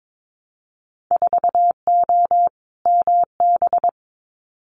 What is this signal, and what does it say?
Morse code, heard as a keyed tone, about 22 wpm, 700 Hz. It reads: 4O MB
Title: Winter Field Day Exchanges Encore 22wpm